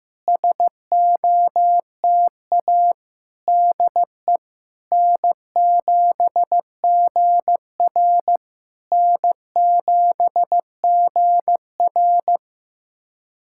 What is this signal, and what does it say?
Morse code, heard as a keyed tone, about 14 wpm, 700 Hz. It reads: SOTA DE N7GR N7GR
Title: SOTA QSO Practice 15wpm